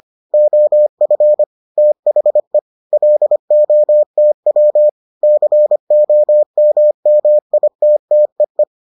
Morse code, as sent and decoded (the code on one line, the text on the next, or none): --- ..-. / - .... / .-.. --- - .-- / -.-. --- -- -- .. - - . .
OF THE LOTW COMMITTEE